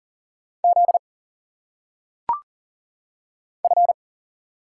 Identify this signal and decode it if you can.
Morse code, heard as a keyed tone, about 44 wpm, 700 Hz. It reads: Z F
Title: Single Letters - Phonetic Pronunciation 40wpm